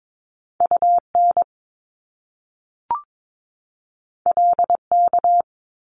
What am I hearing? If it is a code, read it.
Morse code, heard as a keyed tone, about 22 wpm, 700 Hz. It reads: UN LK
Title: Top Two Letter Combinations in Words 22wpm